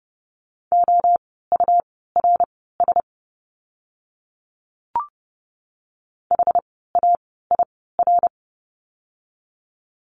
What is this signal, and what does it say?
Morse code, heard as a keyed tone, about 30 wpm, spaced as for 17 wpm, 700 Hz. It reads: OURS HAIR
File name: Lesson 17f - Letter u - Words 30wpm